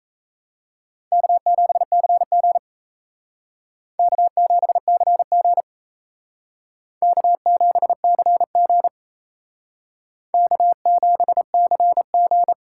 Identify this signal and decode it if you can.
Morse code, heard as a keyed tone, about 34 wpm, 700 Hz. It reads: K7CG K7CG K7CG K7CG
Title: Speed Racing - USA 1x2 Call Signs 28wpm